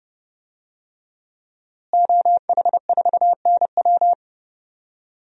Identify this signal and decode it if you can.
Morse code, heard as a keyed tone, about 29 wpm, 700 Hz. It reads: OH4NW
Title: DXCC-100 Call Signs - 5 Characters - Repeated 3x - 30wpm